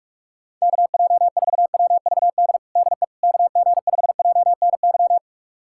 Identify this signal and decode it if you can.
Morse code, heard as a keyed tone, about 43 wpm, 700 Hz. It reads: KJ4WVD DE KG5JNY
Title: USA Callsign DE Callsign - 6 Characters - 45wpm